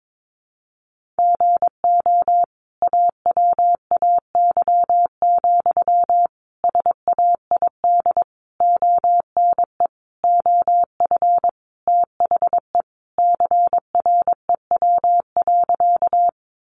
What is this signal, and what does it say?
Morse code, heard as a keyed tone, about 22 wpm, 700 Hz. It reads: GO AWAY, SAID ONE OF THE CREW.
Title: Sentences from Top 1500 Words 22wpm